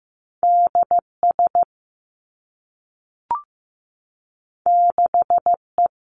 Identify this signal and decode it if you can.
Morse code, heard as a keyed tone, about 15 wpm, 700 Hz. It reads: D S 6 E
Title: Two Challenging Letter-Number Combinations 15wpm